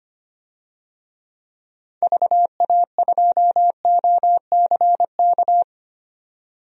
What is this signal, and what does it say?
Morse code, heard as a keyed tone, about 25 wpm, 700 Hz. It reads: VA2OCK
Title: DXCC-100 Call Signs - 6 Characters - 25wpm